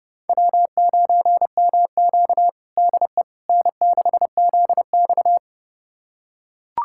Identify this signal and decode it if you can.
Morse code, heard as a keyed tone, about 30 wpm, 700 Hz. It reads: W9MQ DE N6ZX
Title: USA Callsign DE Callsign - 4 Characters - 30wpm